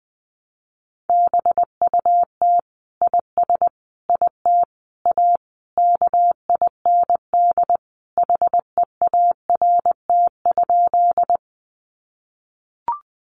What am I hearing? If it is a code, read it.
Morse code, heard as a keyed tone, about 20 wpm, 700 Hz. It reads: BUT IS IT A KIND HEART?
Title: Sentences from Top 1200 Words 20wpm